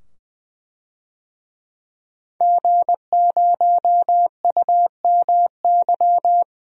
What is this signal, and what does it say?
Morse code, heard as a keyed tone, about 21 wpm, 700 Hz. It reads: G0UMY
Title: DXCC-100 Call Signs - 5 Characters - 20wpm